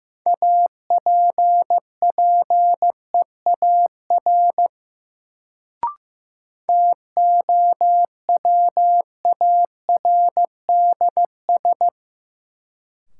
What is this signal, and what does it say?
Morse code, heard as a keyed tone, about 15 wpm, 700 Hz. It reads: APPEAR TOWARDS
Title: Sentences from Top 1000 Words - Review Part 2 - New Word Drill - 15wpm